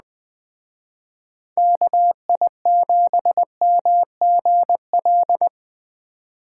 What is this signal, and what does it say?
Morse code, heard as a keyed tone, about 20 wpm, 700 Hz. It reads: KI7MGL
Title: Speed Racing - USA 2x3 Call Signs 17wpm